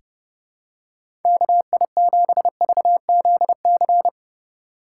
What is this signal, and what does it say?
Morse code, heard as a keyed tone, about 30 wpm, 700 Hz. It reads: KI7VZC
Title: USA Call Signs - 6 Characters - 2x Character Spacing - 30wpm